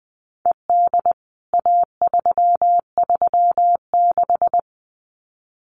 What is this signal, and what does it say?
Morse code, heard as a keyed tone, about 20 wpm, 700 Hz. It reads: ED A336
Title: CWT Exchange 20wpm